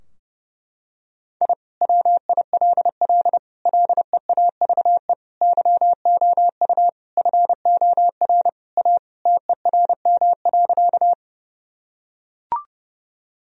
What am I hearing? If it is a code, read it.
Morse code, heard as a keyed tone, about 30 wpm, 700 Hz. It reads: I WILL LEAVE YOU FOR A TERM.
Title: Sentences from Top 1000 Words - Review Part 4 - 30wpm